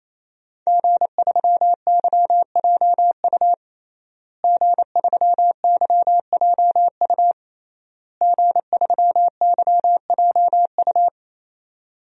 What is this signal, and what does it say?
Morse code, heard as a keyed tone, about 28 wpm, 700 Hz. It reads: G3YJU G3YJU G3YJU
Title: DXCC-100 Call Signs - 5 Characters - Repeated 3x - 28wpm